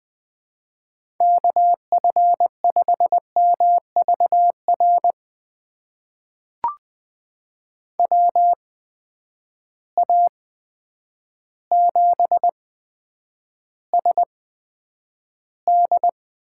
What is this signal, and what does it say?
Morse code, heard as a keyed tone, about 21 wpm, spaced as for 9 wpm, 700 Hz. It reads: KF5MVR WA7SD
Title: USA Call Signs - 6 Characters - 8x Character Spacing - 20wpm